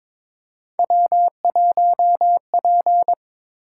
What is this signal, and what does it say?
Morse code, heard as a keyed tone, about 22 wpm, 700 Hz. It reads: W1P
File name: Three Letter-Number Combinations - 22wpm